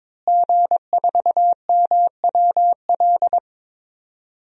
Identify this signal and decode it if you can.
Morse code, heard as a keyed tone, about 22 wpm, 700 Hz. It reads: G4MWL
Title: DXCC-100 Call Signs - 5 Characters - Repeated 2x - 22wpm